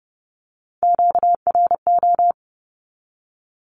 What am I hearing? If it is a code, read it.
Morse code, heard as a keyed tone, about 30 wpm, 700 Hz. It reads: QRO
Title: Q-Codes 30wpm